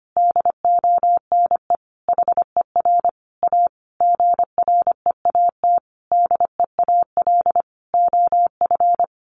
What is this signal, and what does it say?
Morse code, heard as a keyed tone, about 25 wpm, 700 Hz. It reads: DONE HER A GREAT DEAL OF
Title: Sentences from Top 900 Words 25wpm